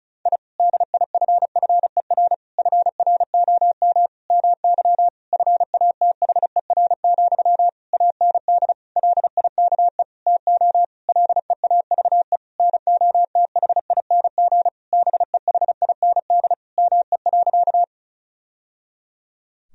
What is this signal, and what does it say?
Morse code, heard as a keyed tone, about 36 wpm, 700 Hz. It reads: I DIFFER FROM MY FATHER, AND LIKE TO LEAVE NOTHING BEHIND ME.
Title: Sentences from Top 1000 Words - Review Part 1 - 35wpm